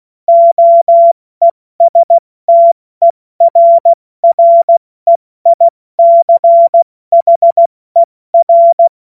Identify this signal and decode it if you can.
Morse code, heard as a keyed tone, about 16 wpm, 680 Hz. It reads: OESTERREICHER
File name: Morsekurs / Witze / Witz 30. Dezember 2025 70 F80